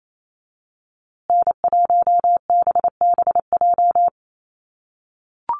Morse code, heard as a keyed tone, about 28 wpm, 700 Hz. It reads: N1BBJ
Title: USA Call Signs Repeated 3x 28wpm